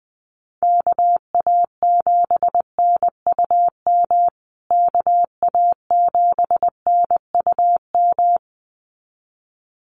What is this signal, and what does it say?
Morse code, heard as a keyed tone, about 20 wpm, 700 Hz. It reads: KA7NUM KA7NUM
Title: USA Call Signs Repeated 2x 20wpm